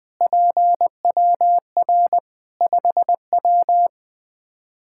Morse code, heard as a keyed tone, about 20 wpm, 700 Hz. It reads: PWR 5W
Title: Practice QSO - Exchange 2 - Info - 20wpm